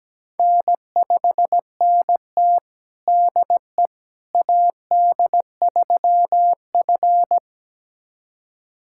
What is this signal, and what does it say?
Morse code, heard as a keyed tone, about 17 wpm, 700 Hz. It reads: N5NT DE AD3F
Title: USA Callsign DE Callsign - 4 Characters - 17wpm